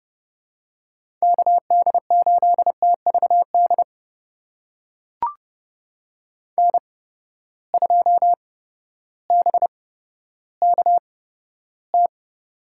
Morse code, heard as a keyed tone, about 30 wpm, spaced as for 14 wpm, 700 Hz. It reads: KD8TVD N2BKT
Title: USA Call Signs - 8x Character Spacing - 30wpm